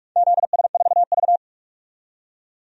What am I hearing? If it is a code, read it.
Morse code, heard as a keyed tone, about 42 wpm, 700 Hz. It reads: ZS4V
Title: DXCC-100 Call Signs - 4 Characters - 45wpm